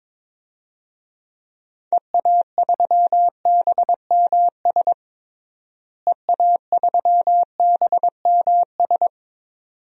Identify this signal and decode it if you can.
Morse code, heard as a keyed tone, about 22 wpm, 700 Hz. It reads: EA3BMS EA3BMS
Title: DXCC-100 Call Signs - 6 Characters - Repeated 2x - 22wpm